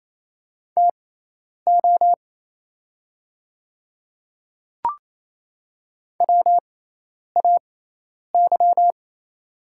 Morse code, heard as a keyed tone, about 28 wpm, spaced as for 10 wpm, 700 Hz. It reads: TO WAY
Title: 6x Character Spacing - Top 100 Words - 28wpm